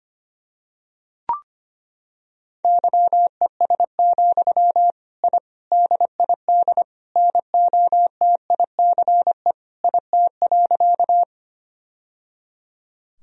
Morse code, heard as a keyed tone, about 25 wpm, 700 Hz. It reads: YES, I DID NOTICE IT.
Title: Sentences from Top 1000 Words - Review Part 2 - 25wpm